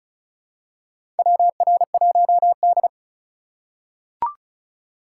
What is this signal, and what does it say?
Morse code, heard as a keyed tone, about 36 wpm, 700 Hz. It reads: WR1D
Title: USA Call Signs - 2x Character Spacing - 35wpm